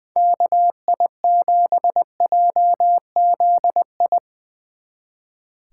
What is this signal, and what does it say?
Morse code, heard as a keyed tone, about 22 wpm, 700 Hz. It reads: KI7JZI
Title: Speed Racing - USA 2x3 Call Signs 20wpm